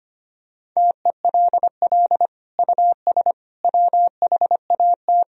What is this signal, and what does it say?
Morse code, heard as a keyed tone, about 25 wpm, 700 Hz. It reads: TELL US WHAT
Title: Sentences from Top 1000 Words - Review Part 4 - 25wpm